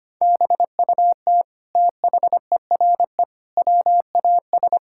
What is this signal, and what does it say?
Morse code, heard as a keyed tone, about 25 wpm, 700 Hz. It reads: BUT THERE WAS
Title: Sentences from Top 1600 Words 25wpm